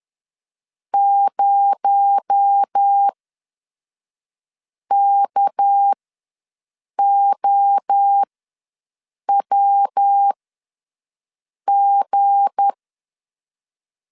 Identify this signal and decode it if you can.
Morse code, heard as a keyed tone, about 10 wpm, spaced as for 5 wpm, 800 Hz. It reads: 0KOWG
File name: LEZIONI / 7 GF9Z / 7 GF9 ws 10wpm 4034